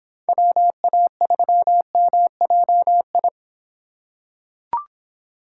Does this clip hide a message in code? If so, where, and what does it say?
Yes: Morse code from 0.3 to 3.3 s, heard as a keyed tone, about 25 wpm, 700 Hz. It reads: WA3MJI